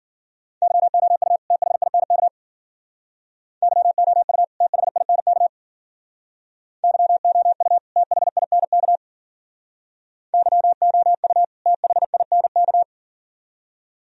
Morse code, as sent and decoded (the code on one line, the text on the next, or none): -.-- --- ..- / - .... .. -. -.- / -.-- --- ..- / - .... .. -. -.- / -.-- --- ..- / - .... .. -. -.- / -.-- --- ..- / - .... .. -. -.-
YOU THINK YOU THINK YOU THINK YOU THINK